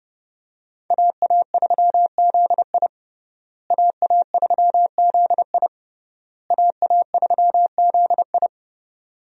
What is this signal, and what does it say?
Morse code, heard as a keyed tone, about 32 wpm, 700 Hz. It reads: AA3ZI AA3ZI AA3ZI
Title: USA Call Signs - 5 Characters - Repeated 3x - 30wpm